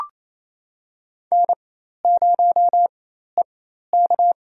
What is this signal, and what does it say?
Morse code, heard as a keyed tone, about 28 wpm, spaced as for 13 wpm, 700 Hz. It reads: N0EK
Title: USA Call Signs - 4x Character Spacing - 28wpm